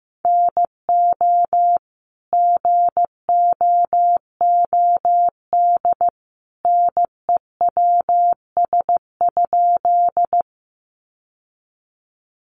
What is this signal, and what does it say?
Morse code, heard as a keyed tone, about 15 wpm, 700 Hz. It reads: NO GOOD NEWS?
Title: Sentences from Top 800 Words 15wpm